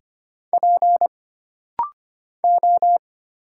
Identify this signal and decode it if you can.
Morse code, heard as a keyed tone, about 25 wpm, 700 Hz. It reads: P O
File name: Single Character (Letter:Number:Punctuation:Prosign) - Rapid-Fire - 25wpm